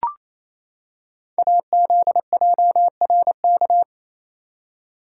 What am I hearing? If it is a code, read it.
Morse code, heard as a keyed tone, about 29 wpm, 700 Hz. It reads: AZJRK
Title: Code Groups - 5 Letter:Numbers - 28wpm